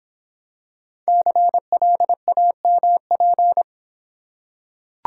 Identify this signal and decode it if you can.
Morse code, heard as a keyed tone, about 26 wpm, 700 Hz. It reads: CLAMP